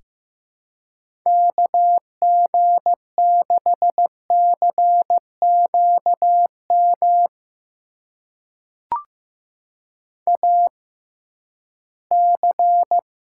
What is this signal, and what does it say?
Morse code, heard as a keyed tone, about 15 wpm, spaced as for 11 wpm, 700 Hz. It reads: KG6CQM AC
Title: USA Call Signs - 6x Character Spacing - 15wpm